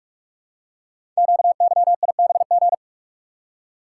Morse code, heard as a keyed tone, about 44 wpm, 700 Hz. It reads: QYIBG
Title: Code Groups - 5 Letters - 45wpm